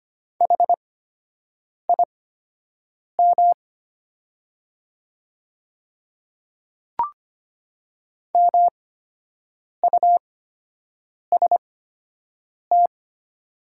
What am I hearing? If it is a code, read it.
Morse code, heard as a keyed tone, about 25 wpm, spaced as for 7 wpm, 690 Hz. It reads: HIM MUST
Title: 8x Character Spacing - Top 200 Words - 25wpm